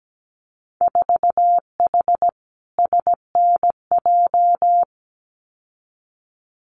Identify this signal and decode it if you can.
Morse code, heard as a keyed tone, about 17 wpm, 700 Hz. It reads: 4H SNJ
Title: Winter Field Day Exchanges Encore 17wpm